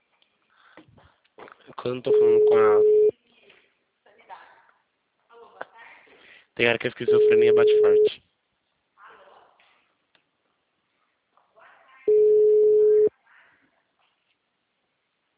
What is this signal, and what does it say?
Ringback tone (Europe, 425 Hz), with speech.
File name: out- -2011- - - .10563